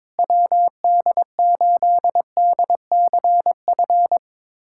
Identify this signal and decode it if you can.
Morse code, heard as a keyed tone, about 22 wpm, 700 Hz. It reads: WD8DCF
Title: USA Callsign DE Callsign - 6 Characters - 22wpm